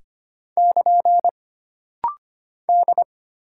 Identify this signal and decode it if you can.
Morse code, heard as a keyed tone, about 25 wpm, 700 Hz. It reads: <KN> D
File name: Single Character (Letter:Number:Punctuation:Prosign) - Rapid-Fire - 25wpm